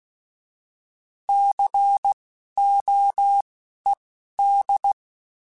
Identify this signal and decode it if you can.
Morse code, heard as a keyed tone, about 16 wpm, spaced as for 11 wpm, 800 Hz. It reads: COED
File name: Alphabetical / Alphabetical mp3s / Cs @10